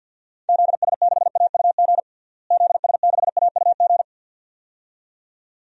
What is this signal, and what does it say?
Morse code, heard as a keyed tone, about 52 wpm, 700 Hz. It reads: ZS6AUG ZS6AUG
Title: DXCC-100 Call Signs - 6 Characters - Repeated 2x - 50wpm